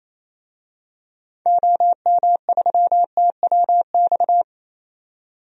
Morse code, heard as a keyed tone, about 28 wpm, 700 Hz. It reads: OM3TWX
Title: DXCC-100 Call Signs - 6 Characters - 28wpm